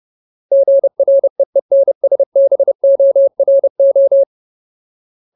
Morse code, heard as a keyed tone, about 30 wpm, 550 Hz. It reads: GREENSBORO
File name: Ultimate Word List / 30WPM Ultimate